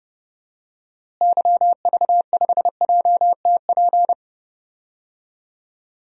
Morse code, heard as a keyed tone, about 30 wpm, 700 Hz. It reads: YV5JTP
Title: DXCC-100 Call Signs - 6 Characters - Repeated 3x - 30wpm